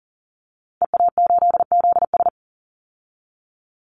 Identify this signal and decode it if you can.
Morse code, heard as a keyed tone, about 40 wpm, 700 Hz. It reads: EA8ZS
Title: Call Signs Repeated 2x - 40wpm